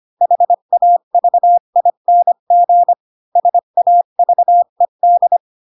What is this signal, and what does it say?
Morse code, heard as a keyed tone, about 25 wpm, 700 Hz. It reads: HAVING SAVED
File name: Pinocchio / Pinocchio ch5 20 25 ik1hge